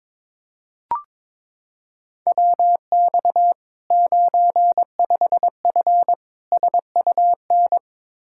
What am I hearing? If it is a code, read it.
Morse code, heard as a keyed tone, about 22 wpm, 700 Hz. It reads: WX 95F SUN
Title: Practice QSO - Exchange 2 - Info - 22wpm